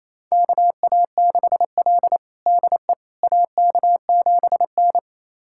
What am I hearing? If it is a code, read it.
Morse code, heard as a keyed tone, about 28 wpm, 700 Hz. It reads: KA6L DE AK7N
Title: USA Callsign DE Callsign - 4 Characters - 28wpm